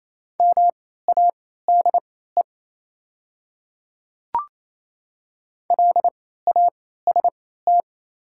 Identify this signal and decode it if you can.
Morse code, heard as a keyed tone, about 28 wpm, spaced as for 16 wpm, 700 Hz. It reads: MADE LAST
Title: 3x Character Spacing - Top 300 Words - 28wpm